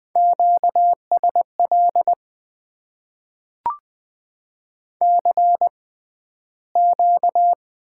Morse code, heard as a keyed tone, about 20 wpm, spaced as for 12 wpm, 700 Hz. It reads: QSL CQ
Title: Top 90 QSO Elements - 6x Character Spacing - 20wpm